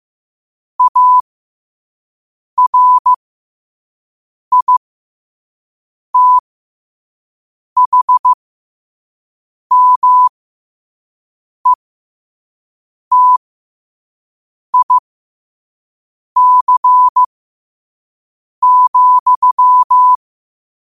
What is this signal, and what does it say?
Morse code, heard as a keyed tone, about 15 wpm, spaced as for 5 wpm, 1000 Hz. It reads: ARITHMETIC,
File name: qotdfiles / 5wpm